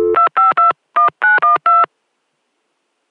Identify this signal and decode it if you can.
DTMF, heard as telephone touch tones, tones of about 140 ms. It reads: 2521912